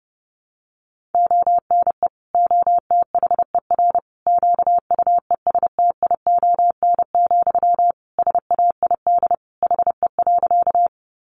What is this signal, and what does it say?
Morse code, heard as a keyed tone, about 30 wpm, 700 Hz. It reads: ONE OTHER QUESTION, SAID HE.
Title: Sentences from Top 400 Words 30wpm